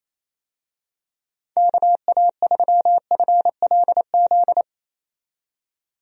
Morse code, heard as a keyed tone, about 28 wpm, 700 Hz. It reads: KA3FLZ
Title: USA Call Signs - 6 Characters - 28wpm